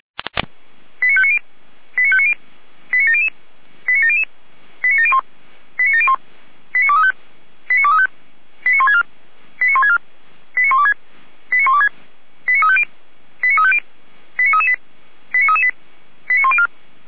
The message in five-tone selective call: 89590 89590 89700 89700 89701 89701 89235 89235 89164 89164 89126 89126 89350 89350 89338 89338 89104

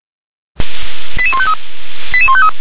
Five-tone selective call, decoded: 88153 88153

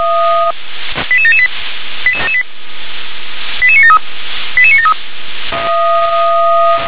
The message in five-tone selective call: 88778 88973 88973

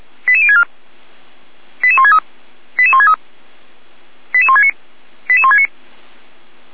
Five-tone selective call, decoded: 88064 88163 88163 88169 88169

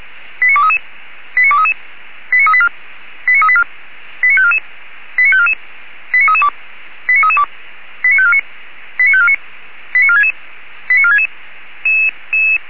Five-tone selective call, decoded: 89233 89233 89384 89384 89544 89544 89332 89332 89549 89549 89466 89466

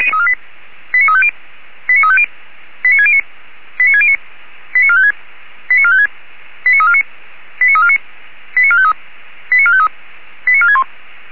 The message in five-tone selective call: 89350 89350 89609 89609 89456 89456 89349 89349 89453 89453 89461

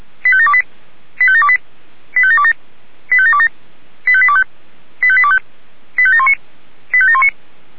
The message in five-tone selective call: 85728 85728 85727 85727 85825 85825 85719 85719

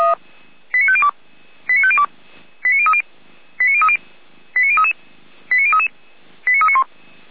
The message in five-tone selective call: 89502 89502 89030 89030 89033 89033 89391

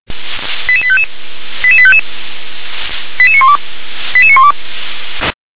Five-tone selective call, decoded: 88855 88855 88912 88912

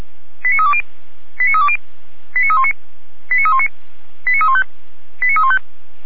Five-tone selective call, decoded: 89320 89320 89319 89319 89315 89315

